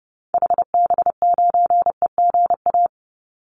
Morse code, heard as a keyed tone, about 30 wpm, 700 Hz. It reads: HB9EGA
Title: Call Signs - 30wpm